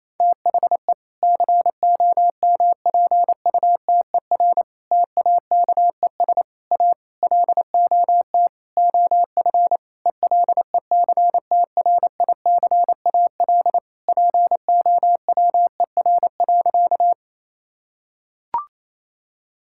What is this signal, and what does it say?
Morse code, heard as a keyed tone, about 28 wpm, 700 Hz. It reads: THE COMPUTER TAKES A LOT OF ELECTRICAL POWER.
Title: Sentences from Top 1000 Words - Review Part 6 - 28wpm